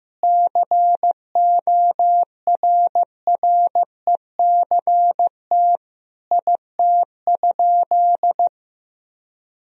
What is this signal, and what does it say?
Morse code, heard as a keyed tone, about 16 wpm, 700 Hz. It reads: CORRECT IT?
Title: Sentences from Top 1000 Words - Review Part 2 - 15wpm